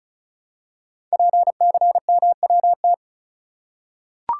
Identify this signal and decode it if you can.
Morse code, heard as a keyed tone, about 36 wpm, 700 Hz. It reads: PCMWT
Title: Code Groups - 5 Letters - 35wpm